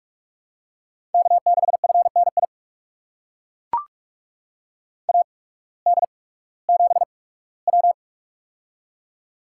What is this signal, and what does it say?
Morse code, heard as a keyed tone, about 42 wpm, spaced as for 20 wpm, 700 Hz. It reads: K6FNI AD7W
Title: USA Call Signs - 8x Character Spacing - 45wpm